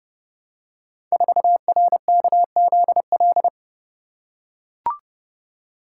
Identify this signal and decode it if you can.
Morse code, heard as a keyed tone, about 30 wpm, 700 Hz. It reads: 4RKZL
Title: Code Groups - 5 Letter:Numbers - 30wpm